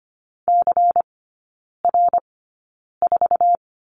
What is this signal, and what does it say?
Morse code, heard as a keyed tone, about 25 wpm, spaced as for 9 wpm, 700 Hz. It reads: CR4
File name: course / 25:10wpm / 093 Lesson 16e - Letter c - Three Characters 25wpm